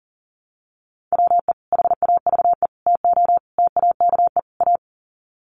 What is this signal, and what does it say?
Morse code, heard as a keyed tone, about 37 wpm, 700 Hz. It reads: WE HAVE TO TAKE A
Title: Sets of 5 Words Encore 40wpm